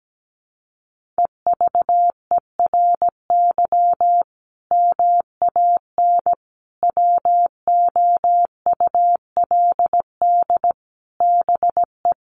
Morse code, heard as a keyed tone, about 17 wpm, 700 Hz. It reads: EVERY MAN WOULD BE
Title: Sentences from Top 1500 Words 17wpm